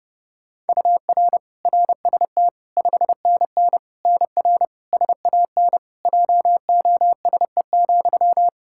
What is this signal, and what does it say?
Morse code, heard as a keyed tone, about 30 wpm, 700 Hz. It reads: UR RST 5NN NR SAN JOSE,
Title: Practice QSO - Exchange 1 - Beginning - 30wpm